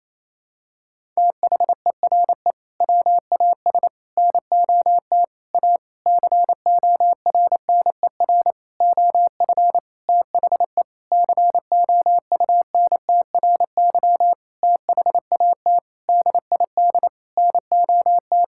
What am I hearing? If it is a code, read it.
Morse code, heard as a keyed tone, about 28 wpm, 700 Hz. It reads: THERE WAS NOT A CORNER OF THE COUNTRY THAT DID NOT
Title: Sentences from Top 1600 Words 28wpm